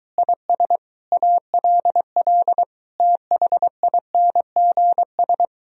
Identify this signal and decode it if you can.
Morse code, heard as a keyed tone, about 22 wpm, 700 Hz. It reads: IS ALL THINGS